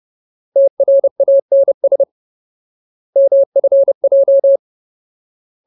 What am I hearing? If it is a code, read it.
Morse code, heard as a keyed tone, about 30 wpm, 550 Hz. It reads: TRANS MFJ
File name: Ultimate Word List / 30WPM Ultimate